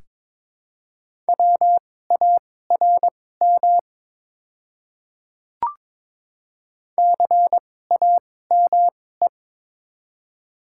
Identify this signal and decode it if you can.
Morse code, heard as a keyed tone, about 21 wpm, spaced as for 16 wpm, 690 Hz. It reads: WARM CAME
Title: 2x Character Spacing - Top 500 Words - 22wpm